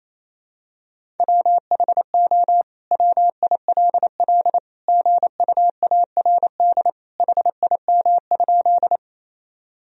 Morse code, heard as a keyed tone, about 28 wpm, 700 Hz. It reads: WHO WILL GUARD HIM?
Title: Sentences from Top 1000 Words - Review Part 4 - 28wpm